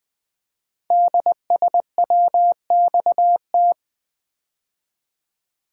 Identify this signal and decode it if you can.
Morse code, heard as a keyed tone, about 20 wpm, 700 Hz. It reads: DSWXT
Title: Code Groups - 5 Letters - 20wpm